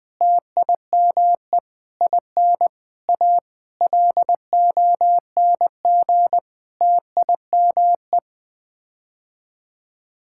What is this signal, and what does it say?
Morse code, heard as a keyed tone, about 20 wpm, 700 Hz. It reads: TIME IN A LONG TIME
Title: Sets of 8 Words - 20wpm